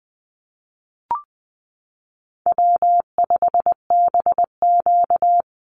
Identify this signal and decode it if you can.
Morse code, heard as a keyed tone, about 20 wpm, 700 Hz. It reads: W5BQ
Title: Call Signs Repeated 3x - 20wpm